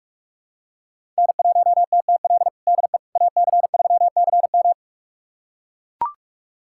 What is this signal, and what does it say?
Morse code, heard as a keyed tone, about 41 wpm, 700 Hz. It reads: N1TTL DE AC3CM
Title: USA Callsign DE Callsign - 5 Characters - 45wpm